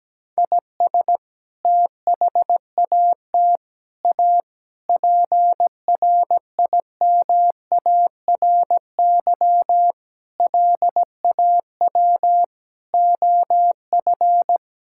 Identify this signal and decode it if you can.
Morse code, heard as a keyed tone, about 17 wpm, 700 Hz. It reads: IS THAT A PRIMARY LAW OF
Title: Sentences from Top 1000 Words - Review Part 5 - 17wpm